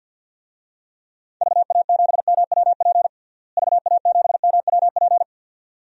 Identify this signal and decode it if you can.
Morse code, heard as a keyed tone, about 50 wpm, 700 Hz. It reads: VA7MWP VA7MWP
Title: DXCC-100 Call Signs - 6 Characters - Repeated 2x - 50wpm